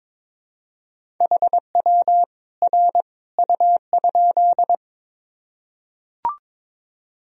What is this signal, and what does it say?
Morse code, heard as a keyed tone, about 22 wpm, 700 Hz. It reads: HW R U?
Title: Sentences from Top 100 Words and Abbreviations 22wpm